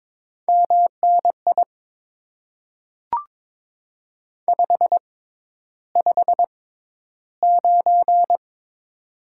Morse code, heard as a keyed tone, about 22 wpm, spaced as for 11 wpm, 700 Hz. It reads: MNI 559
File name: Top 90 QSO Elements - 6x Character Spacing - 22wpm